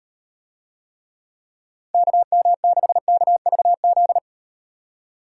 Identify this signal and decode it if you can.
Morse code, heard as a keyed tone, about 38 wpm, 700 Hz. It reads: KM6KVZ